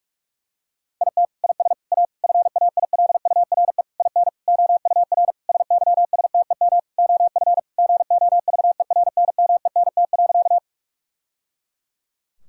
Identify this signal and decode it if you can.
Morse code, heard as a keyed tone, about 46 wpm, 700 Hz. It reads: IT IS A FAILURE IN OUR SYSTEM OF GOVERNMENT.
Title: Sentences from Top 1000 Words - Review Part 6 - 45wpm